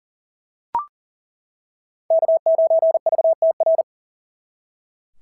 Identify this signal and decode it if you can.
Morse code, heard as a keyed tone, about 40 wpm, 690 Hz. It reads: K9VTR
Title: SOTA QSO Practice 40wpm